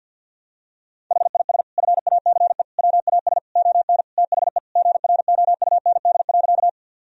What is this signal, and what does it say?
Morse code, heard as a keyed tone, about 50 wpm, 700 Hz. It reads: HIS FACE WAS ON THE GROUND.
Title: Sentences from Top 1000 Words - Review Part 2 - 50wpm